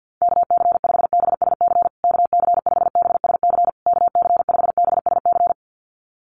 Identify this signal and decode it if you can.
Morse code, heard as a keyed tone, about 50 wpm, 700 Hz. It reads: KC5BSC KC5BSC KC5BSC
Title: USA Call Signs Repeated 3x 50wpm